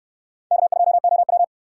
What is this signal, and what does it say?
Morse code, heard as a keyed tone, about 64 wpm, 700 Hz. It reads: K2OW